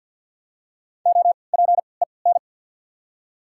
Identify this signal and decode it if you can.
Morse code, heard as a keyed tone, about 54 wpm, 690 Hz. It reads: O P E N
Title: 3x Character Spacing - Top 300 Words - 50wpm